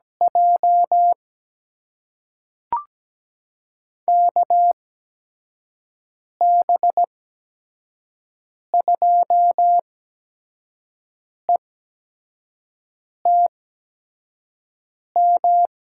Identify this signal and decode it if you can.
Morse code, heard as a keyed tone, about 17 wpm, spaced as for 5 wpm, 700 Hz. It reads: J KB2ETM
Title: USA Call Signs - 6 Characters - 8x Character Spacing - 17wpm